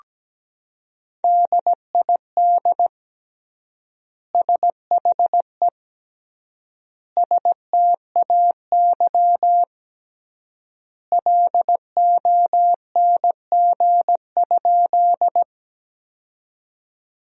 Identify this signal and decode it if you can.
Morse code, heard as a keyed tone, about 17 wpm, 700 Hz. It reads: DID SHE STAY LONG?
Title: 3x Word-Spacing - Sentences from Top 500 Words - 17wpm